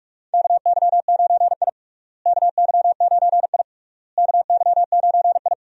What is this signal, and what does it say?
Morse code, heard as a keyed tone, about 44 wpm, 700 Hz. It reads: KY9I KY9I KY9I
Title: USA Call Signs - 4 Characters - Repeated 3x - 45wpm